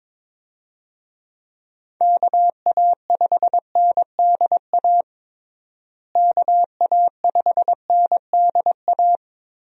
Morse code, heard as a keyed tone, about 22 wpm, 700 Hz. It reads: KA5NDA KA5NDA
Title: USA Call Signs - 6 Characters - Repeated 3x - 22wpm